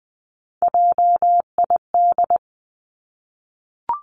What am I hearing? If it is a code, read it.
Morse code, heard as a keyed tone, about 20 wpm, 700 Hz. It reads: JID
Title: Three Challenging Letter-Number Combinations 20wpm